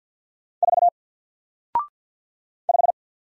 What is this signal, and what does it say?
Morse code, heard as a keyed tone, about 48 wpm, 700 Hz. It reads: 4 5
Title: Single Character (Letter:Number:Punctuation:Prosign) Alt 50wpm